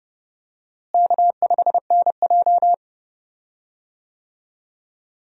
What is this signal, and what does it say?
Morse code, heard as a keyed tone, about 30 wpm, 700 Hz. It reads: K5NJ
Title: USA Call Signs - 4 Characters - Repeated 3x - 30wpm